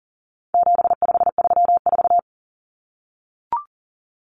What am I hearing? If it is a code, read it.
Morse code, heard as a keyed tone, about 40 wpm, 700 Hz. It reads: 7534
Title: Common Cut Numbers - 4-Digit 40wpm